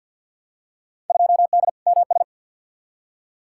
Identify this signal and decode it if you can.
Morse code, heard as a keyed tone, about 52 wpm, 700 Hz. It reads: 2D MS
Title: Field Day Exchanges Encore - 50wpm